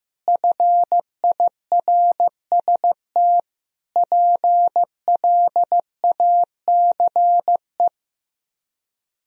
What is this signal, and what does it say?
Morse code, heard as a keyed tone, about 16 wpm, 700 Hz. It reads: FIRST PLACE
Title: Speed-Racing - Sets of 4 Words 15wpm